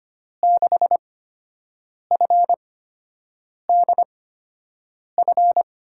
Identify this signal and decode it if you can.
Morse code, heard as a keyed tone, about 25 wpm, spaced as for 7 wpm, 700 Hz. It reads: 6FDF
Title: USA Call Signs - 6 Characters - 8x Character Spacing - 25wpm